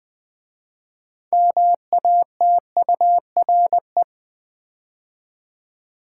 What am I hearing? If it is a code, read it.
Morse code, heard as a keyed tone, about 20 wpm, 700 Hz. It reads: MATURE
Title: 6 Letter Words - First 3 Letters Given - 20wpm